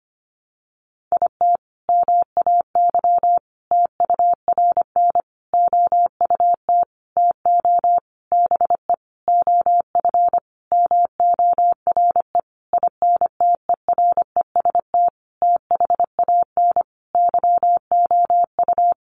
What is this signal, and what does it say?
Morse code, heard as a keyed tone, about 25 wpm, 700 Hz. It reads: IT MAY TURN OUT TO BE OF MORE INTEREST THAN YOU
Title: Sentences from Top 400 Words 25wpm